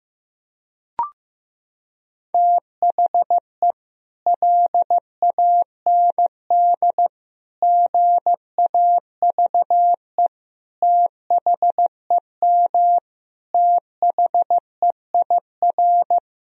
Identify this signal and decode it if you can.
Morse code, heard as a keyed tone, about 15 wpm, 700 Hz. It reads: THE LAND GAVE THEM THEIR
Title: Sentences from Top 1000 Words - Review Part 1 - 15wpm